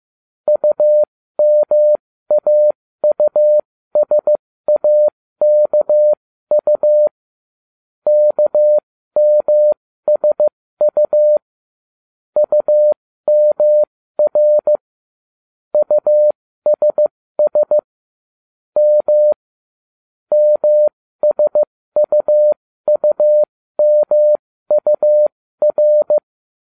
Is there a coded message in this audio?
Yes: Morse code from 0.5 to 26.2 s, heard as a keyed tone, about 15 wpm, spaced as for 13 wpm, 600 Hz. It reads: UMAUSAKU KMSU UMR USS M MSUUMUR